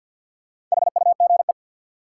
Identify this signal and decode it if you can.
Morse code, heard as a keyed tone, about 53 wpm, 700 Hz. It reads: HUGE
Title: Sentences from Top 1100 Words - New Word Drill - 50wpm